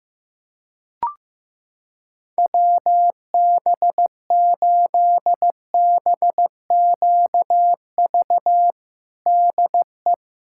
Morse code, heard as a keyed tone, about 15 wpm, 700 Hz. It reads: WB8BQV DE
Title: USA Callsign DE Callsign 15wpm